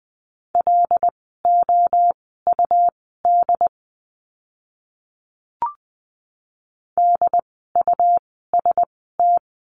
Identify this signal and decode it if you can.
Morse code, heard as a keyed tone, about 20 wpm, spaced as for 15 wpm, 700 Hz. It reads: LOUD DUST
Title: 101 Lesson 17f - Letter u - Words 20wpm